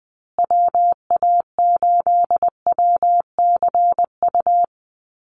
Morse code, heard as a keyed tone, about 20 wpm, 700 Hz. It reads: WA8WCU
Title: USA Call Signs Repeated 3x 20wpm